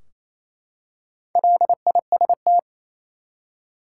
Morse code, heard as a keyed tone, about 28 wpm, 700 Hz. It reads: LIST
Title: Four Letter Words - 28wpm